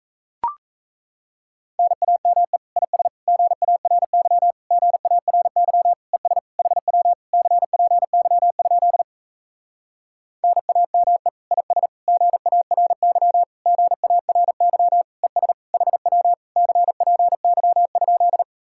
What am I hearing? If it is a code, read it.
Morse code, heard as a keyed tone, about 37 wpm, 700 Hz. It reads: NAME IS GARY GARY ES HW CPY? NAME IS GARY GARY ES HW CPY?